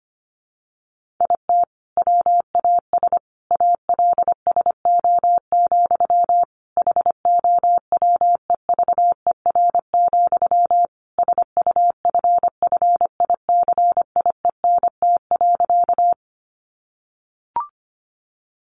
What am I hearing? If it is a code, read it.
Morse code, heard as a keyed tone, about 25 wpm, 700 Hz. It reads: IT WAS ALSO, HOWEVER, SUFFICIENT.
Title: Sentences from Top 1400 Words 25wpm